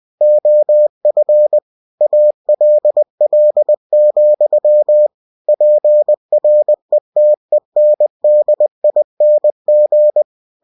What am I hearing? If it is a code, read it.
Morse code, heard as a keyed tone, about 20 wpm, 600 Hz. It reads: OF ALL, PRETENDING
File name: cw-books / winnie-the-pooh / winnie-20wpm-page-0036